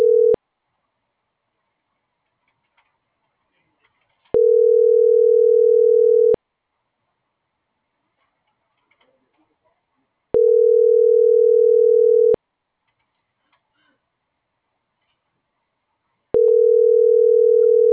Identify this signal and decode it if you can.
Ringback tone (North America), 440 + 480 Hz.